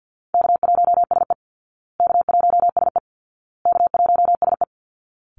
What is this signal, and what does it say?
Morse code, heard as a keyed tone, about 48 wpm, 700 Hz. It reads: K1SE K1SE K1SE
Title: Call Signs Repeated 2x - 50wpm